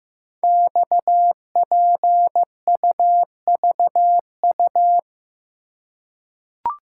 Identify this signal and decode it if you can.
Morse code, heard as a keyed tone, about 15 wpm, 700 Hz. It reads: XPUVU
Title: Code Groups - 5 Letters - 15wpm